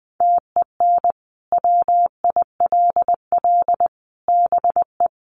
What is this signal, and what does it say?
Morse code, heard as a keyed tone, about 20 wpm, 700 Hz. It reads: TEN WILL BE
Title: Sentences from Top 400 Words 20wpm